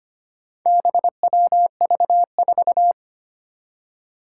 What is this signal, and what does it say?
Morse code, heard as a keyed tone, about 25 wpm, 700 Hz. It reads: BWV4
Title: Four Challenging Letter-Number Combinations 25wpm